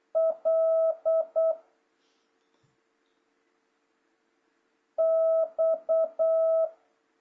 Morse code, heard as a keyed tone, about 8 wpm, 650 Hz. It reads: L X
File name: jh8cbh / 2026cw